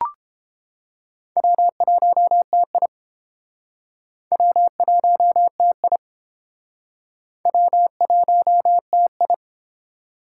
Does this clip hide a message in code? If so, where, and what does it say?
Yes: Morse code from 1.4 to 9.3 s, heard as a keyed tone, about 30 wpm, 700 Hz. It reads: W1TI W1TI W1TI